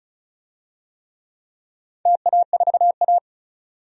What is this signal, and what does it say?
Morse code, heard as a keyed tone, about 36 wpm, 700 Hz. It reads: TA4A
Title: DXCC-100 Call Signs - 4 Characters - 35wpm